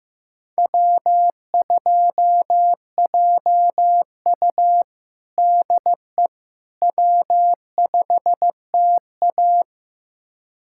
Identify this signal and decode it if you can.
Morse code, heard as a keyed tone, about 15 wpm, 700 Hz. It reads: W2JU DE W5TA
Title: USA Callsign DE Callsign - 4 Characters - 15wpm